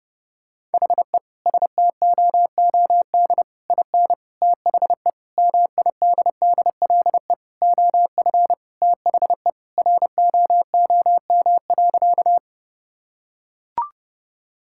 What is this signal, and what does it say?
Morse code, heard as a keyed tone, about 30 wpm, 700 Hz. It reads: HE STOOD IN THE MIDDLE OF THE ROOM.
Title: Sentences from Top 1000 Words - Review Part 6 - 30wpm